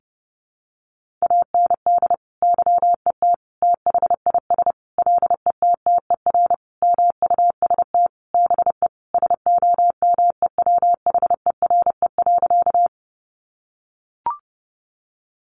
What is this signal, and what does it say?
Morse code, heard as a keyed tone, about 30 wpm, 700 Hz. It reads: AND YET THIS LETTER MUST BE SOMEWHERE.
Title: Sentences from Top 1400 Words 30wpm